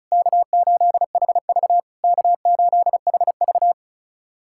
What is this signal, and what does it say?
Morse code, heard as a keyed tone, about 36 wpm, 690 Hz. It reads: K8HV K8HV
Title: SOTA QSO Practice 35wpm